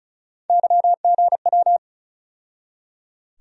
Morse code, heard as a keyed tone, about 34 wpm, 700 Hz. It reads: YGW
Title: USA Call Sign Postfixes - 3 Letters 35wpm